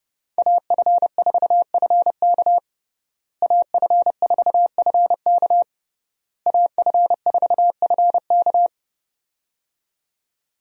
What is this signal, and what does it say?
Morse code, heard as a keyed tone, about 30 wpm, 700 Hz. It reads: AF4FK AF4FK AF4FK
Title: USA Call Signs - 5 Characters - Repeated 3x - 30wpm